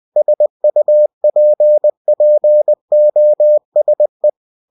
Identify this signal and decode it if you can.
Morse code, heard as a keyed tone, about 20 wpm, 600 Hz. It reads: SUPPOSE
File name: cw-books / winnie-the-pooh / winnie-20wpm-page-0041